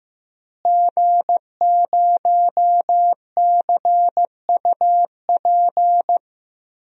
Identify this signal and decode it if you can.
Morse code, heard as a keyed tone, about 15 wpm, 700 Hz. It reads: G0CUP
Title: DXCC-100 Call Signs - 5 Characters - Repeated 3x - 15wpm